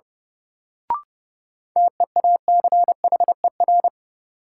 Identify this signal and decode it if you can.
Morse code, heard as a keyed tone, about 31 wpm, 700 Hz. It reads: TEACHER
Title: Sentences from Top 900 Words - New Word Drill - 30wpm